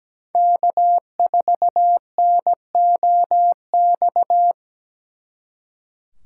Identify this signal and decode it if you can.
Morse code, heard as a keyed tone, about 20 wpm, 700 Hz. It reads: K4NOX
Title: Speed Racing - USA Call Signs 17wpm